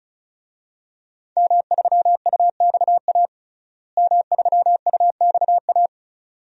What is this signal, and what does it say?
Morse code, heard as a keyed tone, about 36 wpm, 700 Hz. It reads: M3UXA M3UXA
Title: DXCC-100 Call Signs - 5 Characters - Repeated 2x - 35wpm